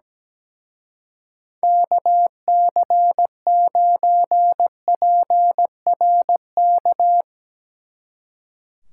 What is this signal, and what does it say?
Morse code, heard as a keyed tone, about 19 wpm, 700 Hz. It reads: KC9PRK
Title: Speed Racing - USA 2x3 Call Signs 17wpm